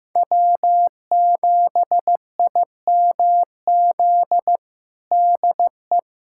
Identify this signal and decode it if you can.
Morse code, heard as a keyed tone, about 15 wpm, 700 Hz. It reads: W7IMZ DE
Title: USA Callsign DE Callsign - 5 Characters - 15wpm